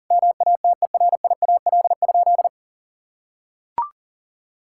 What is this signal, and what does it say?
Morse code, heard as a keyed tone, about 40 wpm, 700 Hz. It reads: MATERIAL?
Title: Sentences from Top 1000 Words - Review Part 5 - 40wpm